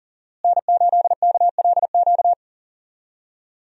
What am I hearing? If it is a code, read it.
Morse code, heard as a keyed tone, about 40 wpm, 700 Hz. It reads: N8KLQ
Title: USA Call Signs - 5 Characters - Repeated 3x - 40wpm